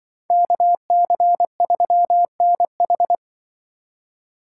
Morse code, heard as a keyed tone, about 24 wpm, 700 Hz. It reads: KC3NH
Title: Speed Racing - USA 2x2 Call Signs 20wpm